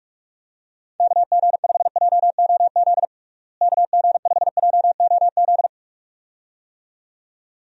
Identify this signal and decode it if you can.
Morse code, heard as a keyed tone, about 46 wpm, 700 Hz. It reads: KG5JOZ KG5JOZ
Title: USA Call Signs - 6 Characters - Repeated 2x - 45wpm